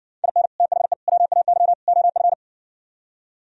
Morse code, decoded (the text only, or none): IN THE WAY OF